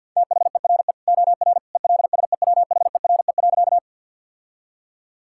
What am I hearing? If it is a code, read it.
Morse code, heard as a keyed tone, about 50 wpm, 700 Hz. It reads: THERE OR ELSEWHERE.
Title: Sentences from Top 1600 Words 50wpm